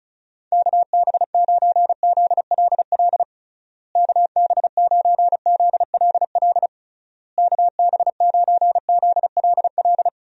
Morse code, heard as a keyed tone, about 36 wpm, 700 Hz. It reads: KB9ZLL KB9ZLL KB9ZLL
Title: USA Call Signs - 6 Characters - Repeated 3x - 35wpm